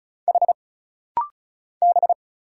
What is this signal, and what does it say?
Morse code, heard as a keyed tone, about 35 wpm, 700 Hz. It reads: H B
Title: Single Letters - Rapid-Fire 35wpm